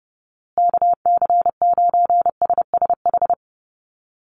Morse code, heard as a keyed tone, about 30 wpm, 700 Hz. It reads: KC9SSH
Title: USA Call Signs Repeated 2x 30wpm